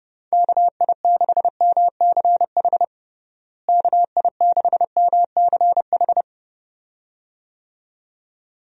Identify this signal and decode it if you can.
Morse code, heard as a keyed tone, about 30 wpm, 700 Hz. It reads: KI6MCH KI6MCH
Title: USA Call Signs - 6 Characters - Repeated 2x - 30wpm